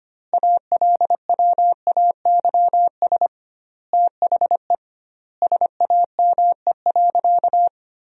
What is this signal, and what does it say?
Morse code, heard as a keyed tone, about 25 wpm, 700 Hz. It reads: ALWAYS THE SAME.
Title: 2x Word-Spacing - Sentences from Top 300 Words - 25wpm